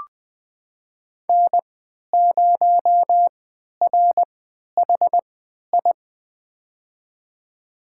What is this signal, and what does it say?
Morse code, heard as a keyed tone, about 20 wpm, spaced as for 11 wpm, 700 Hz. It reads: N0RHI
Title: USA Call Signs - 3x Character Spacing - 20wpm